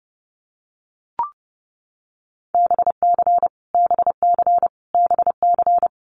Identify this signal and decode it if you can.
Morse code, heard as a keyed tone, about 30 wpm, 700 Hz. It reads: BC BC BC
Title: Canadian Provinces and Territories - Repeated 3x - 30wpm